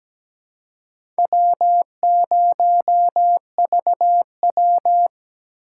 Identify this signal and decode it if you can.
Morse code, heard as a keyed tone, about 17 wpm, 700 Hz. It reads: W0VW
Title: USA Call Signs - 8x Character Spacing - 17wpm